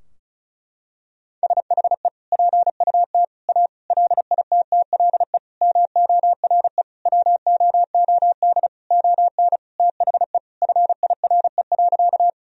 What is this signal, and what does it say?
Morse code, heard as a keyed tone, about 34 wpm, 700 Hz. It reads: SHE PUT A LITTLE MORE WOOD ON THE FIRE.
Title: Sentences from Top 1000 Words - Review Part 2 - 35wpm